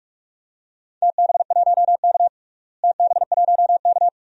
Morse code, heard as a keyed tone, about 46 wpm, 700 Hz. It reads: TB1K TB1K
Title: DXCC-100 Call Signs - 4 Characters - Repeated 2x - 45wpm